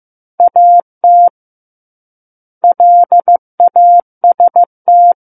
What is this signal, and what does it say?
Morse code, heard as a keyed tone, about 15 wpm, 700 Hz. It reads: AT LAST
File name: Chapter 6 - 15 WPM